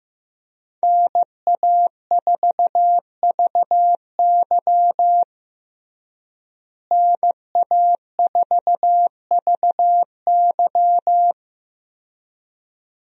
Morse code, heard as a keyed tone, about 16 wpm, 700 Hz. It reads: NA4VY NA4VY
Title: USA Call Signs - 5 Characters - Repeated 2x - 15wpm